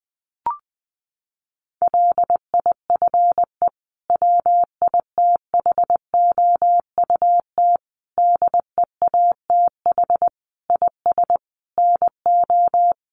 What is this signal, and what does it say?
Morse code, heard as a keyed tone, about 20 wpm, 700 Hz. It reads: LIFE WITHOUT DEATH IS NO
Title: Sentences from Top 1100 Words 20wpm